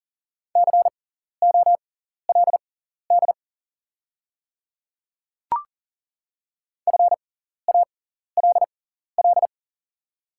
Morse code, heard as a keyed tone, about 40 wpm, spaced as for 14 wpm, 690 Hz. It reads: COLD FALL
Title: 6x Character Spacing - Top 500 Words - 40wpm